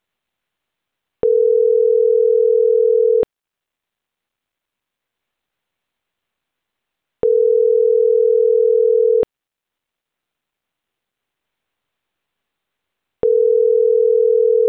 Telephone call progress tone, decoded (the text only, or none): ringback tone